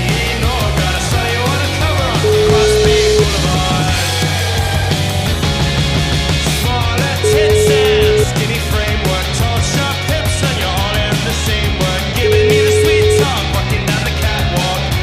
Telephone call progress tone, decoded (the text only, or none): ringback tone